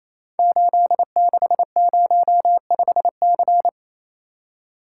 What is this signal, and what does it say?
Morse code, heard as a keyed tone, about 29 wpm, 700 Hz. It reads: 8605C
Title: Code Groups - 5 Letter:Numbers - 28wpm